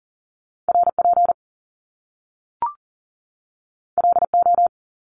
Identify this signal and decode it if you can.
Morse code, heard as a keyed tone, about 41 wpm, 700 Hz. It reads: RP LO
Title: Top Two Letter Combinations in Words 40wpm